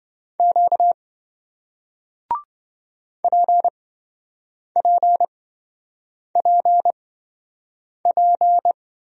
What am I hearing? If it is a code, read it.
Morse code, heard as a keyed tone, about 27 wpm, spaced as for 7 wpm, 700 Hz. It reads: Q PPPP